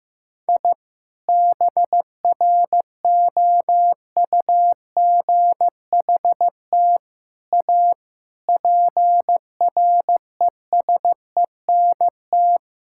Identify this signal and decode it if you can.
Morse code, heard as a keyed tone, about 15 wpm, 700 Hz. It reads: I BROUGHT A PRESENT
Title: Sentences from Top 1000 Words - Review Part 5 - 15wpm